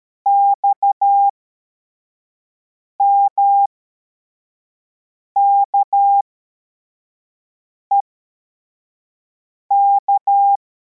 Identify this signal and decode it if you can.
Morse code, heard as a keyed tone, about 13 wpm, spaced as for 4 wpm, 800 Hz. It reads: XMKEK